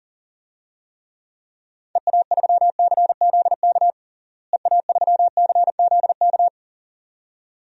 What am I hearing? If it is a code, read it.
Morse code, heard as a keyed tone, about 40 wpm, 700 Hz. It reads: EA3CZK EA3CZK
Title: DXCC-100 Call Signs - 6 Characters - Repeated 2x - 40wpm